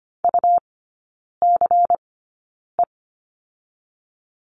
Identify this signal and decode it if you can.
Morse code, heard as a keyed tone, about 25 wpm, 700 Hz. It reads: U C E
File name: Lesson 17e - Letter u - Three Characters 25wpm